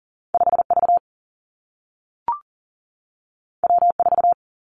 Morse code, heard as a keyed tone, about 41 wpm, 700 Hz. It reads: 5V W4
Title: Two Challenging Letter-Number Combinations 40wpm